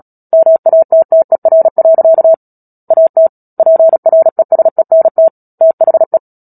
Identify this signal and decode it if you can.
Morse code, heard as a keyed tone, about 37 wpm, 650 Hz. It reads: MATTER. AT PRESENT THE
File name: Book Two / 10-Chapter 10 - Epilogue - 36 WPM